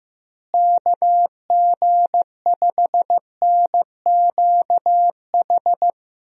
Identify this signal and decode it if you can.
Morse code, heard as a keyed tone, about 15 wpm, 700 Hz. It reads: KG5NQH
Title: USA Call Signs - 6 Characters - Repeated 2x - 15wpm